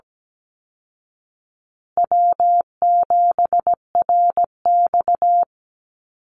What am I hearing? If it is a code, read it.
Morse code, heard as a keyed tone, about 17 wpm, 700 Hz. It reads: W7RX
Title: Call Signs Repeated 2x - 17wpm